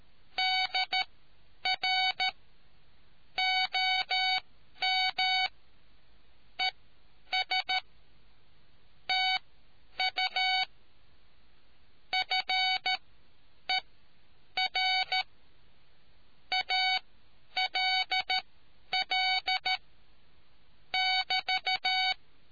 Morse code, heard as a keyed tone, about 13 wpm, spaced as for 9 wpm, 750 Hz. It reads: DR OM ES TU FER ALL <BT>